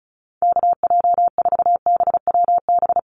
Morse code, heard as a keyed tone, about 34 wpm, 700 Hz. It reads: KJ4BWB
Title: USA Call Signs 35wpm